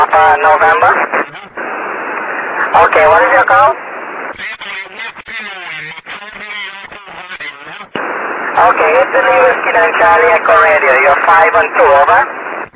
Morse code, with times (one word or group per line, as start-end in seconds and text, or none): none